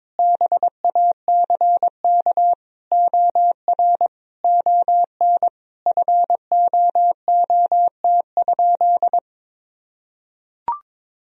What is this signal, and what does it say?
Morse code, heard as a keyed tone, about 22 wpm, 700 Hz. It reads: BACK OR ON FOOT?
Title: Sentences from Top 1000 Words - Review Part 3 - 22wpm